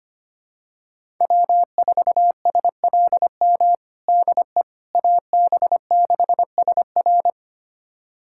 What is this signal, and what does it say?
Morse code, heard as a keyed tone, about 25 wpm, 700 Hz. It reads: W4SLM DE AB6SR
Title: USA Callsign DE Callsign - 5 Characters - 25wpm